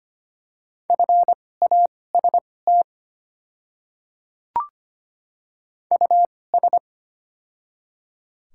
Morse code, heard as a keyed tone, about 25 wpm, spaced as for 18 wpm, 700 Hz. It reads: FAST US
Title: 2x Character Spacing - Top 500 Words - 25wpm